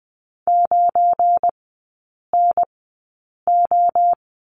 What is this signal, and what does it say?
Morse code, heard as a keyed tone, about 20 wpm, spaced as for 8 wpm, 690 Hz. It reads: 9NO
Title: 172 Lesson 28e - Number 9 - Three Characters 20wpm